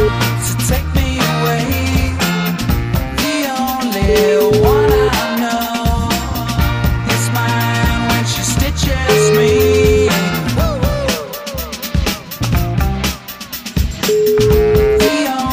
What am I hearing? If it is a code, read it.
Ringback tone (Europe, 425 Hz), with noise.